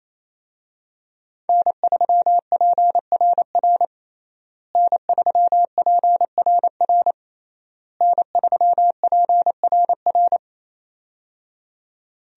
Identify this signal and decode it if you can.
Morse code, heard as a keyed tone, about 28 wpm, 700 Hz. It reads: N3PRR N3PRR N3PRR
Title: USA Call Signs - 5 Characters - Repeated 3x - 28wpm